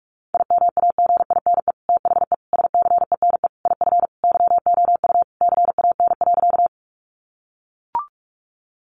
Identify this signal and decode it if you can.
Morse code, heard as a keyed tone, about 44 wpm, 700 Hz. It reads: IMAGINE THE SCENE IF YOU CAN.
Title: Sentences from Top 1400 Words 45wpm